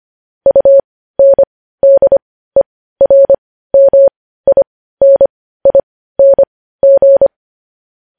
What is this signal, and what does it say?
Morse code, heard as a keyed tone, about 25 wpm, spaced as for 15 wpm, 550 Hz. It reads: UNDERMINING
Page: news-headlines-in-morse-code-at-15-wpm